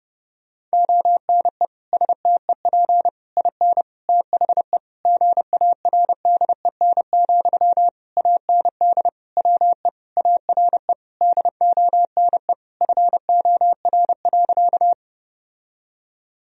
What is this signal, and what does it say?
Morse code, heard as a keyed tone, about 30 wpm, 700 Hz. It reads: ONE STEP IN THE GARDEN, AND WE ARE DONE FOR.
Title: Sentences from Top 1000 Words - Review Part 4 - 30wpm